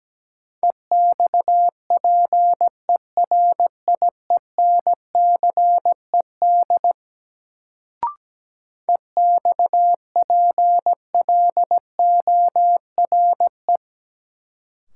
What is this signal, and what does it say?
Morse code, heard as a keyed tone, about 17 wpm, 700 Hz. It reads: EXPERIENCED EXPLORE
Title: Sentences from Top 1100 Words - New Word Drill Alt - 17wpm